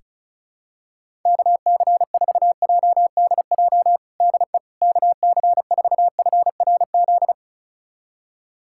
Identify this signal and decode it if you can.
Morse code, heard as a keyed tone, about 36 wpm, 700 Hz. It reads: KC4JDJ DE KC4FRZ
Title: USA Callsign DE Callsign - 6 Characters - 35wpm